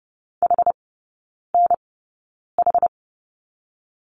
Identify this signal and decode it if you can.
Morse code, heard as a keyed tone, about 30 wpm, spaced as for 9 wpm, 700 Hz. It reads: HNH
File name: Lesson 18e - Letter m - Three Characters 30wpm